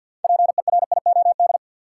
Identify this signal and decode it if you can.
Morse code, heard as a keyed tone, about 46 wpm, 700 Hz. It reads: PERIOD
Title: Sentences from Top 800 Words - New Word Drill - 50wpm